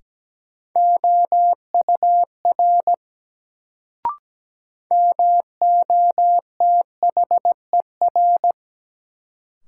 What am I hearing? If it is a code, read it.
Morse code, heard as a keyed tone, about 17 wpm, 700 Hz. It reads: OUR MOTHER
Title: Sentences from Top 200 Words - New Word Drill - 17wpm